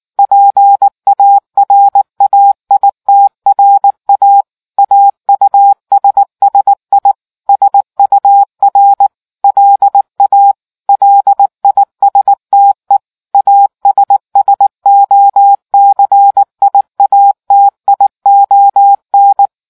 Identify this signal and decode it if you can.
Morse code, heard as a keyed tone, about 19 wpm, 800 Hz. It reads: PARAITRA AUSSI SUR LA LISTE ASSOCIATION